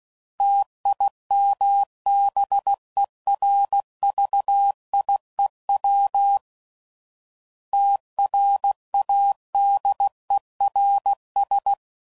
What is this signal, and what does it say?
Morse code, heard as a keyed tone, about 16 wpm, 800 Hz. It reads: TIMBERVIEW TRADERS
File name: CW Code Groups @15WPM / fromSM's / Cities